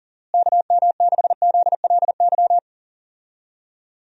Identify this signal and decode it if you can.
Morse code, heard as a keyed tone, about 40 wpm, 700 Hz. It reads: KM6ZLY
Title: USA Call Signs - 6 Characters - Repeated 2x - 40wpm